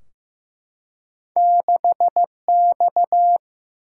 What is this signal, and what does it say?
Morse code, heard as a keyed tone, about 15 wpm, 700 Hz. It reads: 6X
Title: Single-Letter Single-Number 15wpm